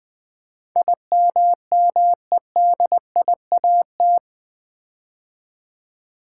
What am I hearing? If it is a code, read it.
Morse code, heard as a keyed tone, about 20 wpm, 700 Hz. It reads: IMMEDIAT
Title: Build A Word - 4 Letters At A Time - 11-Letter Words - 20wpm